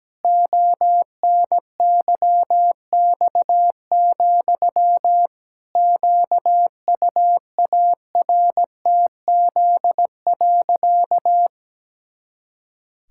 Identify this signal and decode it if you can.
Morse code, heard as a keyed tone, about 17 wpm, 700 Hz. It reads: ONYX, QUARTZ.
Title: Pangram Challenge v2 - 17wpm